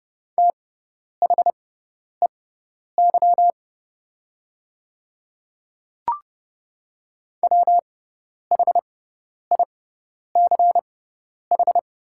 Morse code, heard as a keyed tone, about 30 wpm, spaced as for 10 wpm, 700 Hz. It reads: THEY WHICH
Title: 6x Character Spacing - Top 200 Words - 30wpm